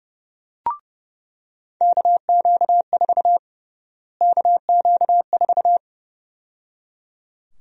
Morse code, heard as a keyed tone, about 30 wpm, 700 Hz. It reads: KQ4 KQ4
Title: USA Call Sign Prefixes - Repeated 2x - 30wpm